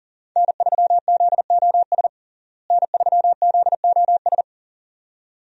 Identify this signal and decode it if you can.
Morse code, heard as a keyed tone, about 40 wpm, 700 Hz. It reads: N3ZOS N3ZOS
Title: USA Call Signs - 5 Characters - Repeated 2x - 40wpm